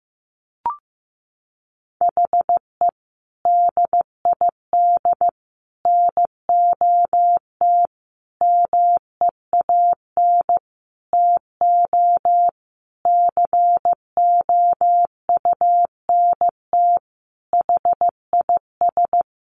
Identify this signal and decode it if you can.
Morse code, heard as a keyed tone, about 15 wpm, 700 Hz. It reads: HE DID NOT MEAN TO COUNT HIS
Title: Sentences from Top 1500 Words 15wpm